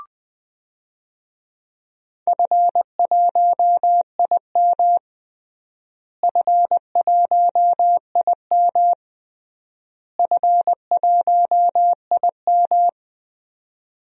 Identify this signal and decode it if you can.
Morse code, heard as a keyed tone, about 20 wpm, 700 Hz. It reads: F1IM F1IM F1IM
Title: DXCC-100 Call Signs - 4 Characters - Repeated 3x - 20wpm